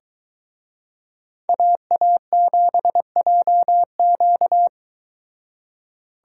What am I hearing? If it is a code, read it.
Morse code, heard as a keyed tone, about 23 wpm, 700 Hz. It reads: AA7JQ